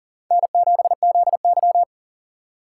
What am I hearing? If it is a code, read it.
Morse code, heard as a keyed tone, about 40 wpm, 700 Hz. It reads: N7ZY
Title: USA Call Signs - 4x Character Spacing - 40wpm